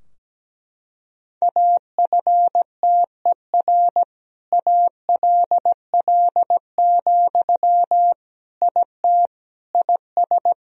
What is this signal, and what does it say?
Morse code, heard as a keyed tone, about 17 wpm, 700 Hz. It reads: AFTER ALL, IT IS
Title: Sentences from Top 1000 Words - Review Part 4 - 17wpm